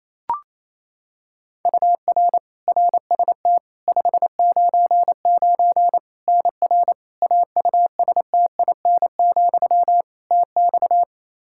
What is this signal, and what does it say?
Morse code, heard as a keyed tone, about 28 wpm, 700 Hz. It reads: UR RST 599 NR AUSTIN, TX
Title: Practice QSO - Exchange 1 - Beginning - 28wpm